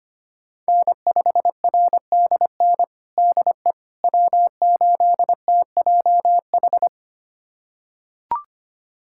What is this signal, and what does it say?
Morse code, heard as a keyed tone, about 25 wpm, 700 Hz. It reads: N5RDN DE W8TJH
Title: USA Callsign DE Callsign - 5 Characters - 25wpm